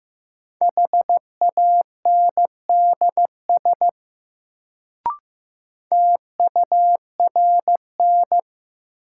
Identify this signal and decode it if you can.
Morse code, heard as a keyed tone, about 15 wpm, 700 Hz. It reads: HANDS TURN
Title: Sets of 4 Words - New Word Drill - 15wpm